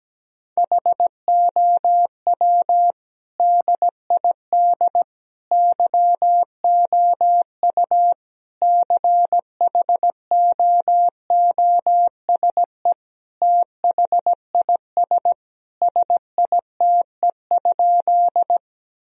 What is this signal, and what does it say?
Morse code, heard as a keyed tone, about 17 wpm, 700 Hz. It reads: HOW DID YOU CHOOSE THIS SITE?
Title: Sentences from Top 1000 Words - Review Part 4 - 17wpm